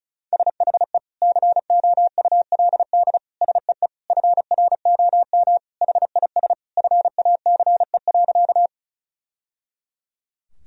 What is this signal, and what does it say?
Morse code, heard as a keyed tone, about 36 wpm, 700 Hz. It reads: SHE COULD SEE FROM HIS FACE.
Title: Sentences from Top 1000 Words - Review Part 2 - 35wpm